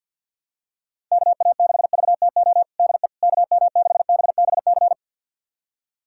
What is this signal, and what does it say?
Morse code, heard as a keyed tone, about 51 wpm, 700 Hz. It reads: KA6VTO DE KM6BBC
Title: USA Callsign DE Callsign 50wpm